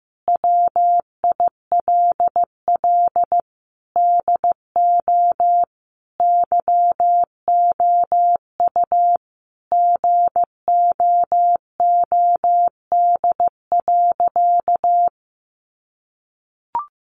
Morse code, heard as a keyed tone, about 15 wpm, 700 Hz. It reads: WILL DO YOU GOOD.
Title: Sentences from Top 1200 Words 15wpm